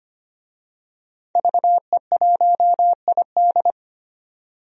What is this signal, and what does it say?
Morse code, heard as a keyed tone, about 25 wpm, 700 Hz. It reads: VE1ID
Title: DXCC-100 Call Signs - 5 Characters - 25wpm